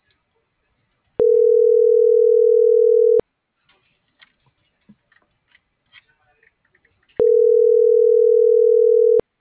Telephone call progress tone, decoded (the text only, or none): ringback tone